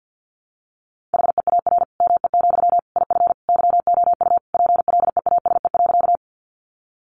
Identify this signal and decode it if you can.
Morse code, heard as a keyed tone, about 50 wpm, 700 Hz. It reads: HEAR ME, IF YOU PLEASE.
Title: Sentences from Top 1500 Words 50wpm